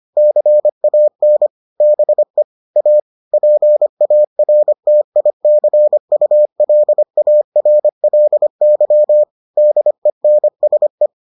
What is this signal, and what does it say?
Morse code, heard as a keyed tone, about 25 wpm, 600 Hz. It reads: CAN BE A PARTICULARLY DENSE